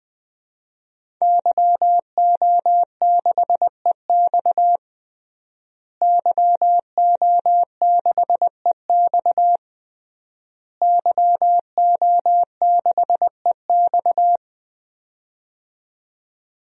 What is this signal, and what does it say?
Morse code, heard as a keyed tone, about 20 wpm, 700 Hz. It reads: YO6EX YO6EX YO6EX
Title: DXCC-100 Call Signs - 5 Characters - Repeated 3x - 20wpm